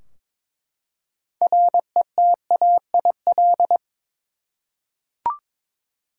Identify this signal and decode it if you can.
Morse code, heard as a keyed tone, about 22 wpm, 700 Hz. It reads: RETAIL
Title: Practice QSO - Exchange 3 - More Info - 22wpm